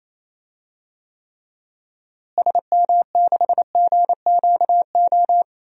Morse code, heard as a keyed tone, about 28 wpm, 700 Hz. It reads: SM6GQO
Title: DXCC-100 Call Signs - 6 Characters - Repeated 2x - 28wpm